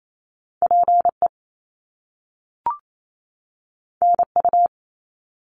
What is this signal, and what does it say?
Morse code, heard as a keyed tone, about 28 wpm, 700 Hz. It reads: PE NU
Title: Canadian Provinces and Territories 28wpm